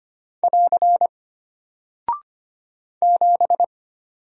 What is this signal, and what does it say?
Morse code, heard as a keyed tone, about 25 wpm, 700 Hz. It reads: <AR> 7
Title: Single Character (Letter:Number:Punctuation:Prosign) Alt 25wpm